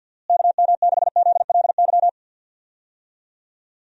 Morse code, heard as a keyed tone, about 53 wpm, 700 Hz. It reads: KM6ZLY
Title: USA Call Signs - 6 Characters - Repeated 2x - 50wpm